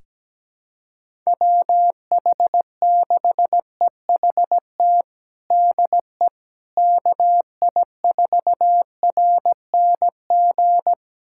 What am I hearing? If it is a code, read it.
Morse code, heard as a keyed tone, about 17 wpm, 700 Hz. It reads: WH6EHT DE KI4RNG
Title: USA Callsign DE Callsign 17wpm